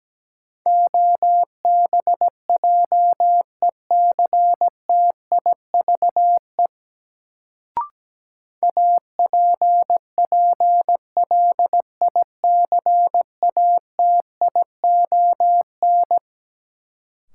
Morse code, heard as a keyed tone, about 17 wpm, 700 Hz. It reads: OBJECTIVE APPLICATION
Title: Sentences from Top 1200 Words - New Word Drill - 17wpm